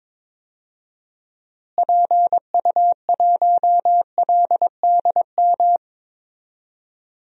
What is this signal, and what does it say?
Morse code, heard as a keyed tone, about 22 wpm, 700 Hz. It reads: PU1LDM
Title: DXCC-100 Call Signs - 6 Characters - Repeated 3x - 22wpm